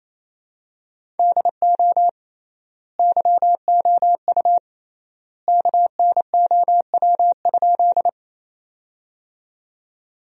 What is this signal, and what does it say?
Morse code, heard as a keyed tone, about 28 wpm, 700 Hz. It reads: DO YOU KNOW?
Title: 3x Word-Spacing - Sentences from Top 100 Words - 28wpm